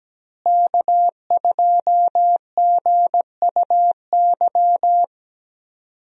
Morse code, heard as a keyed tone, about 17 wpm, 700 Hz. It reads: K2GUY
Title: USA Call Signs - 5 Characters - Repeated 3x - 17wpm